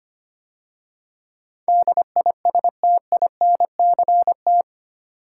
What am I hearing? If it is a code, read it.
Morse code, heard as a keyed tone, about 25 wpm, 700 Hz. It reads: DISTINCT
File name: 8 Letter Words - First 3 Letters Given - 25wpm